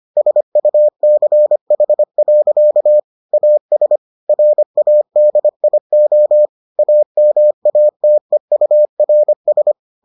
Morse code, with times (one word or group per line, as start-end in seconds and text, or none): 0.2-3.0 SUCH.
3.3-4.0 AS
4.3-6.5 RADIO
6.8-9.7 AMATEURS